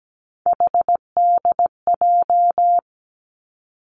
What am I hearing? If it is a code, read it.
Morse code, heard as a keyed tone, about 17 wpm, 700 Hz. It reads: HDJ
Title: Three Challenging Letter-Number Combinations 17wpm